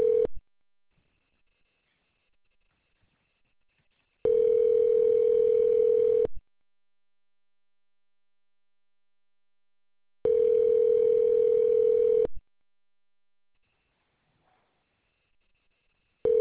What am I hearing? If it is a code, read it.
Ringback tone (North America), 440 + 480 Hz.